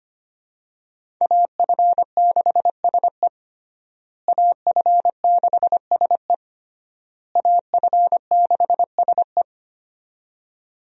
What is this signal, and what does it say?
Morse code, heard as a keyed tone, about 25 wpm, 700 Hz. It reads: AF6SE AF6SE AF6SE
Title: USA Call Signs - 5 Characters - Repeated 3x - 25wpm